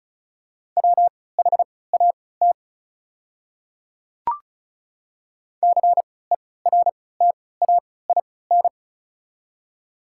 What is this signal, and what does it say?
Morse code, heard as a keyed tone, about 36 wpm, spaced as for 20 wpm, 700 Hz. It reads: WHAT CERTAIN
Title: 3x Character Spacing - Top 500 Words - 35wpm